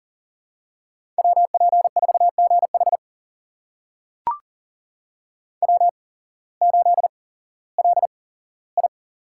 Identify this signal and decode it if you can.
Morse code, heard as a keyed tone, about 40 wpm, spaced as for 19 wpm, 700 Hz. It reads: WP4GH W8LI
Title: USA Call Signs - 8x Character Spacing - 40wpm